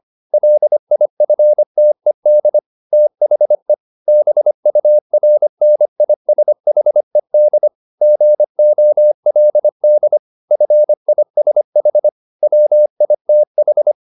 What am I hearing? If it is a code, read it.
Morse code, heard as a keyed tone, about 25 wpm, 600 Hz. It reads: LIFTED THE BURNISHED GOLD FISH WITH